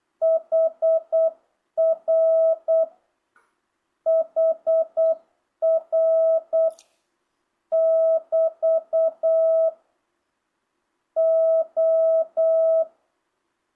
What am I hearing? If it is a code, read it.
Morse code, heard as a keyed tone, about 8 wpm, 660 Hz. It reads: HR HR <BT> O
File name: jh8cbh / 2025cw